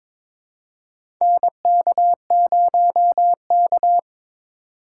Morse code, heard as a keyed tone, about 22 wpm, 700 Hz. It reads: NK0K
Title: USA Call Signs - 4x Character Spacing - 22wpm